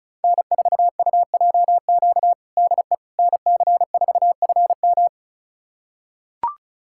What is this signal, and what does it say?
Morse code, heard as a keyed tone, about 34 wpm, 700 Hz. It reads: N4UJQ DE NC4FM
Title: USA Callsign DE Callsign - 5 Characters - 35wpm